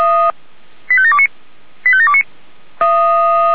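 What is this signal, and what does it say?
Five-tone selective call (ZVEI), heard as quick runs of five tones, tones of about 70 ms. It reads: 85729 85729
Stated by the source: ZVEI five-tone selective call received by SDR.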